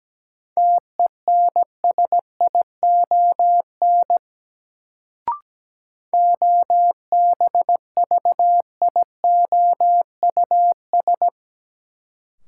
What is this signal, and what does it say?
Morse code, heard as a keyed tone, about 17 wpm, 700 Hz. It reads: TENSION OBVIOUS
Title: Sentences from Top 1600 Words - New Word Drill Alt - 17wpm